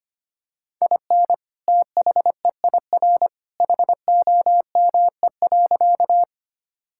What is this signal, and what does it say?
Morse code, heard as a keyed tone, about 25 wpm, 700 Hz. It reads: IN THEIR HOME.
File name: practice / 25wpm / Sentences with 2-Word Phrases - 25wpm